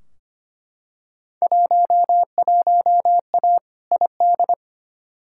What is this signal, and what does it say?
Morse code, heard as a keyed tone, about 25 wpm, 700 Hz. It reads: 11A ID
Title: Field Day Exchanges Encore - 25wpm